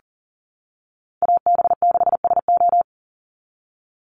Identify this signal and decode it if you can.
Morse code, heard as a keyed tone, about 37 wpm, 700 Hz. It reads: AB6SO
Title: NA SOTA Chaser Callsigns 40wpm